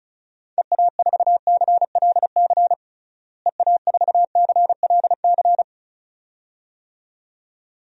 Morse code, heard as a keyed tone, about 36 wpm, 700 Hz. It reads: EA4CLC EA4CLC
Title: DXCC-100 Call Signs - 6 Characters - Repeated 2x - 35wpm